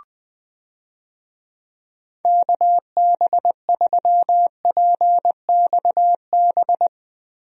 Morse code, heard as a keyed tone, about 20 wpm, 700 Hz. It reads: KB3PXB
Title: USA Call Signs - 6 Characters - Repeated 2x - 20wpm